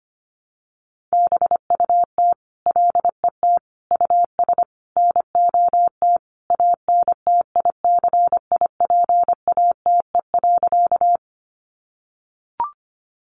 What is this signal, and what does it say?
Morse code, heard as a keyed tone, about 25 wpm, 700 Hz. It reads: BUT LET US NOT ANTICIPATE.
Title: Sentences from Top 1500 Words 25wpm